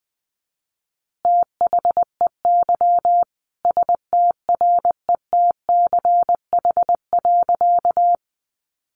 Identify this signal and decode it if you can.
Morse code, heard as a keyed tone, about 20 wpm, 700 Hz. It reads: THEY STRETCH.
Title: Easy Phrases 20wpm